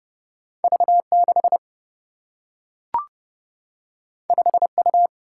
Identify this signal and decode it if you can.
Morse code, heard as a keyed tone, about 30 wpm, 700 Hz. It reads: V6 5U
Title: Single-Letter Single-Number 30wpm